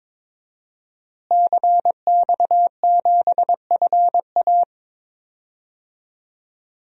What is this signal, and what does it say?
Morse code, heard as a keyed tone, about 22 wpm, 700 Hz. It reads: CX7FA
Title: DXCC-100 Call Signs - 5 Characters - Repeated 3x - 22wpm